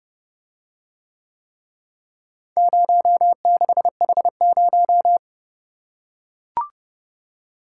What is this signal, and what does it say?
Morse code, heard as a keyed tone, about 28 wpm, 700 Hz. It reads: 06H0